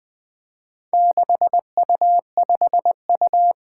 Morse code, heard as a keyed tone, about 20 wpm, 700 Hz. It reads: 6U5U
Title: Four Challenging Letter-Number Combinations 20wpm